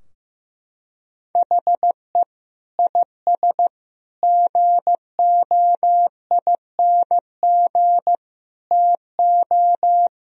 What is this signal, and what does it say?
Morse code, heard as a keyed tone, about 15 wpm, 700 Hz. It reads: HE IS GOING TO
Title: Sentences from Top 1000 Words - Review Part 4 - 15wpm